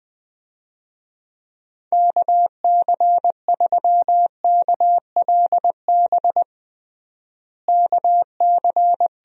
Morse code, heard as a keyed tone, about 21 wpm, 700 Hz. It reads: KC3KLB KC
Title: USA Call Signs - 6 Characters - Repeated 3x - 20wpm